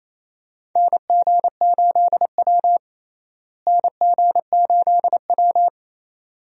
Morse code, heard as a keyed tone, about 28 wpm, 700 Hz. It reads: NG8W NG8W
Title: USA Call Signs - 4 Characters - Repeated 2x - 28wpm